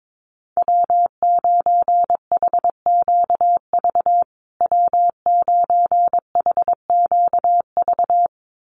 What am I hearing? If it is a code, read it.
Morse code, heard as a keyed tone, about 22 wpm, 700 Hz. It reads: W9HQV W9HQV
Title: USA Call Signs Repeated 2x 22wpm